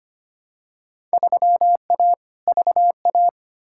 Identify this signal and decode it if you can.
Morse code, heard as a keyed tone, about 25 wpm, 700 Hz. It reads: 3A VA
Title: Field Day Exchanges Encore - 25wpm